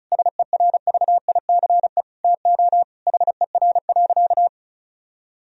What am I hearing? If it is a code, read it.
Morse code, heard as a keyed tone, about 36 wpm, 700 Hz. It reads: SERVICE TO HER.
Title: Sentences from Top 1000 Words - Review Part 4 - 35wpm